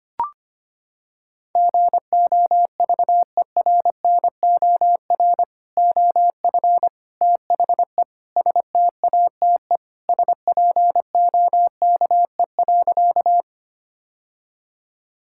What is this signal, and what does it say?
Morse code, heard as a keyed tone, about 25 wpm, 700 Hz. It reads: GOVERNOR OF THE STATE SPOKE.
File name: practice / 25wpm / Sentences with 3-Word Phrases - 25wpm